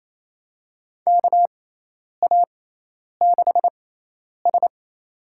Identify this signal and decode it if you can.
Morse code, heard as a keyed tone, about 28 wpm, spaced as for 10 wpm, 700 Hz. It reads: KA6S
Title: USA Call Signs - 4 Characters - 6x Character Spacing - 28wpm